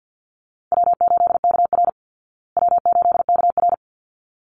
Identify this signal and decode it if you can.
Morse code, heard as a keyed tone, about 50 wpm, 700 Hz. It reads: W8KR W8KR
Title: Easy Call Signs Repeated 2x - 50wpm